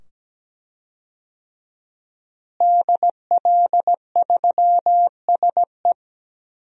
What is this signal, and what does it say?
Morse code, heard as a keyed tone, about 17 wpm, 700 Hz. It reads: DL3SE
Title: DXCC-100 Call Signs - 5 Characters - Repeated 2x - 17wpm